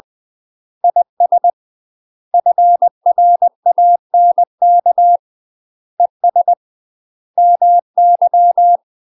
Morse code, heard as a keyed tone, about 20 wpm, 700 Hz. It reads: IS FRANK ES MY
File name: UK MASTER QSOs TRAINING / 20WPM / UK MASTER QSOs 0005